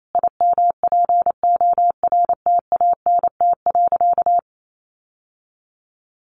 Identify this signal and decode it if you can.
Morse code, heard as a keyed tone, about 28 wpm, 700 Hz. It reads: IMPORTANT.
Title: Sentences from Top 600 Words 28wpm